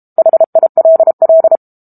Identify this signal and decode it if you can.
Morse code, heard as a keyed tone, about 32 wpm, 650 Hz. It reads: HILL
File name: Book Two / 06-Chapter 6 - 32 WPM